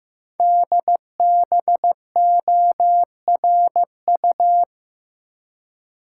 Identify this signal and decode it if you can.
Morse code, heard as a keyed tone, about 15 wpm, 700 Hz. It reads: DBORU
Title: Code Groups - 5 Letters - 15wpm